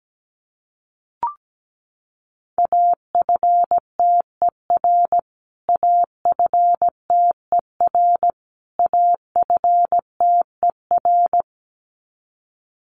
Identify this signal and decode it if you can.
Morse code, heard as a keyed tone, about 17 wpm, 700 Hz. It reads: AFTER AFTER AFTER
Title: Top 500 Words Repeated 17wpm